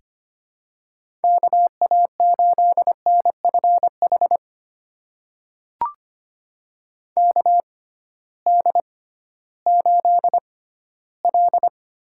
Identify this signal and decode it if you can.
Morse code, heard as a keyed tone, about 25 wpm, spaced as for 15 wpm, 700 Hz. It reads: KA8NFH KD8L
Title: USA Call Signs - 6 Characters - 6x Character Spacing - 25wpm